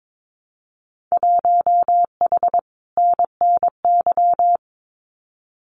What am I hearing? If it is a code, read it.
Morse code, heard as a keyed tone, about 22 wpm, 700 Hz. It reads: 1H NNY
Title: Winter Field Day Exchanges 22wpm